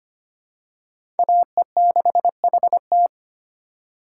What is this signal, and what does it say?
Morse code, heard as a keyed tone, about 25 wpm, 700 Hz. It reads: AE6HT
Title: USA Call Signs - 8x Character Spacing - 25wpm